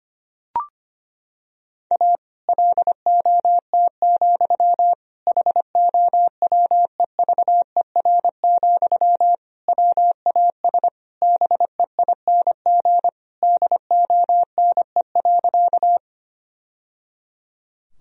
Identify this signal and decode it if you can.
Morse code, heard as a keyed tone, about 25 wpm, 700 Hz. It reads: A LOT, HOWEVER, WAS BEING DONE.
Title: Sentences from Top 1000 Words - Review Part 3 - 25wpm